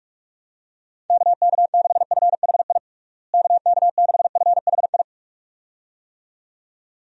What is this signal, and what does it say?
Morse code, heard as a keyed tone, about 43 wpm, 700 Hz. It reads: KK6FHI KK6FHI
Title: USA Call Signs - 6 Characters - Repeated 2x - 45wpm